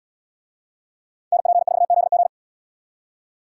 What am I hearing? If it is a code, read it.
Morse code, heard as a keyed tone, about 69 wpm, 700 Hz. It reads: NP4ZG